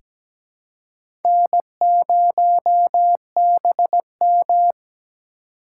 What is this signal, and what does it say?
Morse code, heard as a keyed tone, about 17 wpm, 700 Hz. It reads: N0BM
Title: USA Call Signs - 4 Characters - 4x Character Spacing - 17wpm